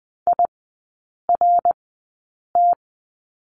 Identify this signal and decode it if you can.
Morse code, heard as a keyed tone, about 20 wpm, spaced as for 8 wpm, 700 Hz. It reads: IRT
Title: 127 Lesson 21e - Number 6 - Three Characters 20wpm